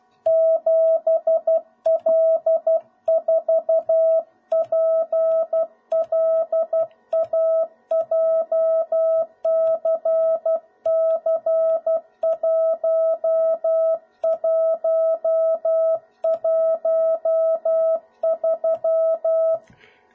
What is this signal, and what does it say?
Morse code, heard as a keyed tone, about 12 wpm, 650 Hz. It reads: 7L4PLA JCC1113
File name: jh8cbh / 2021cw / cw01b